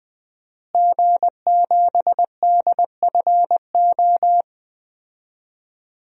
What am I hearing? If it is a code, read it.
Morse code, heard as a keyed tone, about 21 wpm, 700 Hz. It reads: G7DFO
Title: DXCC-100 Call Signs - 5 Characters - Repeated 2x - 20wpm